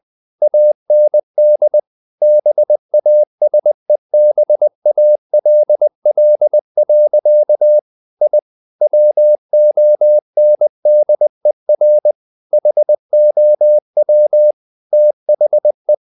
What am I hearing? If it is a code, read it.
Morse code, heard as a keyed tone, about 20 wpm, 600 Hz. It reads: AND BASEBALL. I WONDER HOW THE
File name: the old man and the sea part 2 20wpm